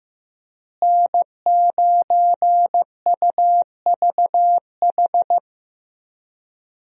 Morse code, heard as a keyed tone, about 15 wpm, 700 Hz. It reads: N9UVH
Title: USA Call Signs - 5 Characters - Repeated 3x - 15wpm